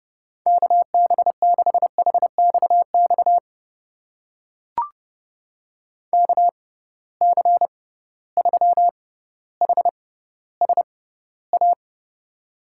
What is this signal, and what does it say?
Morse code, heard as a keyed tone, about 31 wpm, spaced as for 16 wpm, 700 Hz. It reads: KB6HXX KC3HSA
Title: USA Call Signs - 6 Characters - 6x Character Spacing - 30wpm